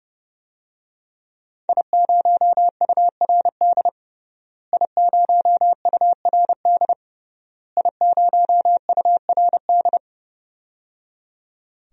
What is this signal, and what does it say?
Morse code, heard as a keyed tone, about 30 wpm, 700 Hz. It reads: I0URD I0URD I0URD
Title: DXCC-100 Call Signs - 5 Characters - Repeated 3x - 30wpm